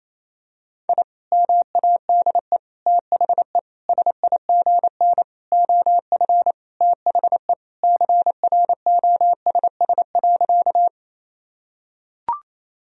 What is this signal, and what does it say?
Morse code, heard as a keyed tone, about 28 wpm, 700 Hz. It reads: I MADE THE SIGN OF THE CROSS.
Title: Sentences from Top 1000 Words - Review Part 1 - 28wpm